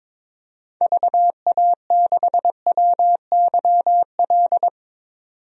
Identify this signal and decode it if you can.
Morse code, heard as a keyed tone, about 22 wpm, 700 Hz. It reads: VA6WYL
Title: DXCC-100 Call Signs - 6 Characters - 22wpm